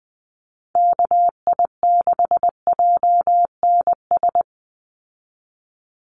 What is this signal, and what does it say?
Morse code, heard as a keyed tone, about 20 wpm, 700 Hz. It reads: KI6JNS
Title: USA Call Signs 20wpm